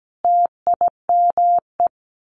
Morse code, heard as a keyed tone, about 17 wpm, 700 Hz. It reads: TIME
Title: A Sound Of Thunder 17wpm